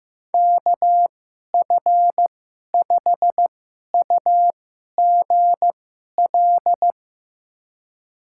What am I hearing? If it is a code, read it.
Morse code, heard as a keyed tone, about 15 wpm, spaced as for 11 wpm, 700 Hz. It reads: KF5UGL
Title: USA Call Signs - 2x Character Spacing - 15wpm